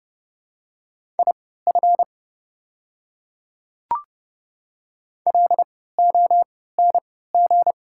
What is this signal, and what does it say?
Morse code, heard as a keyed tone, about 30 wpm, spaced as for 17 wpm, 700 Hz. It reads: IF LONG
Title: 3x Character Spacing - Top 100 Words - 30wpm